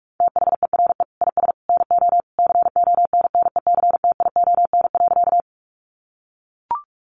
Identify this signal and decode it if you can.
Morse code, heard as a keyed tone, about 44 wpm, 700 Hz. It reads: THERE IS NO CONNECTION.
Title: Sentences from Top 1300 Words 45wpm